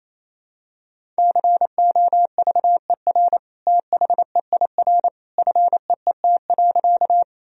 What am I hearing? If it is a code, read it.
Morse code, heard as a keyed tone, about 28 wpm, 700 Hz. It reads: COVER THEIR FEET.
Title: Sentences from Top 1000 Words - Review Part 1 - 28wpm